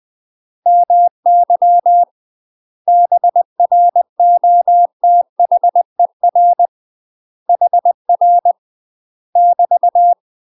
Morse code, heard as a keyed tone, about 20 wpm, 700 Hz. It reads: MY BROTHER HR <BT>
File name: UK MASTER QSOs TRAINING / 20WPM / UK MASTER QSOs 0042